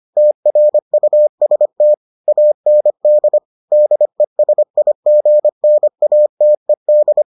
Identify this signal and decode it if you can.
Morse code, heard as a keyed tone, about 25 wpm, 600 Hz. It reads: TRUST AND DESIGNATED